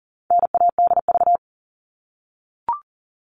Morse code, heard as a keyed tone, about 40 wpm, 700 Hz. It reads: NADV
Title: All Cut Numbers - 4-Digit 40wpm